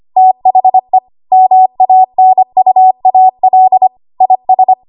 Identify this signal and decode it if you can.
Morse code, heard as a keyed tone, about 25 wpm, 750 Hz. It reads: THE MANUAL IS
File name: Archive / 25 WPM / 25WPM